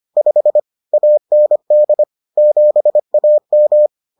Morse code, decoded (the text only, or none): 5 AND 7AM